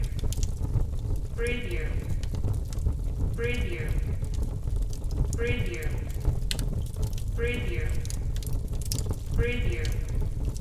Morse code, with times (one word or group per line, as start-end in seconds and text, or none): none